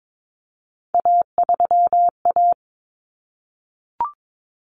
Morse code, heard as a keyed tone, about 22 wpm, 700 Hz. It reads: A3A
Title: Common Cut Numbers - 3-Digit 22wpm